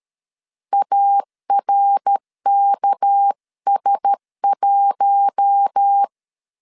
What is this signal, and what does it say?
Morse code, heard as a keyed tone, about 13 wpm, 800 Hz. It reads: ARKS1
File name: CW / DET / 1 6 1500C S B 12wpm 3431